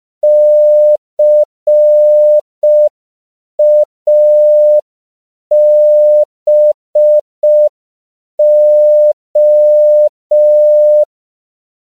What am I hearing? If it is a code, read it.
Morse code, heard as a keyed tone, about 5 wpm, 600 Hz. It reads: CABO
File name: CW / 5PPM